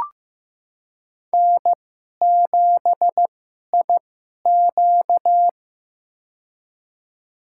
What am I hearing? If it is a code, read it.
Morse code, heard as a keyed tone, about 15 wpm, spaced as for 11 wpm, 700 Hz. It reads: N7IQ
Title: USA Call Signs - 2x Character Spacing - 15wpm